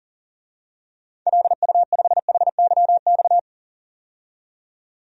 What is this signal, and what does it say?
Morse code, heard as a keyed tone, about 40 wpm, 700 Hz. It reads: LU5HYX
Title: DXCC-100 Call Signs - 6 Characters - 40wpm